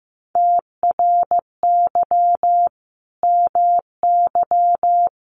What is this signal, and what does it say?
Morse code, heard as a keyed tone, about 15 wpm, 700 Hz. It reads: TRY MY
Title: Sentences from Top 1200 Words 15wpm